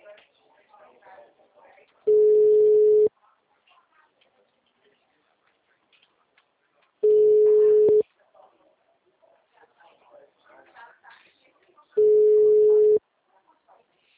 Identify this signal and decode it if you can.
Ringback tone (Europe, 425 Hz), with speech.
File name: out- -2017- - - .27134